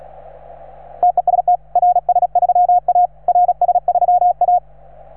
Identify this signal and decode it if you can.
Morse code, heard as a keyed tone, about 36 wpm, 720 Hz. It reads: TEST RS3A RS3A